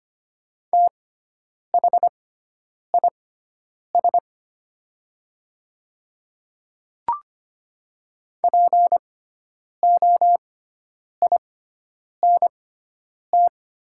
Morse code, heard as a keyed tone, about 25 wpm, spaced as for 9 wpm, 700 Hz. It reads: THIS POINT
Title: 6x Character Spacing - Top 200 Words - 25wpm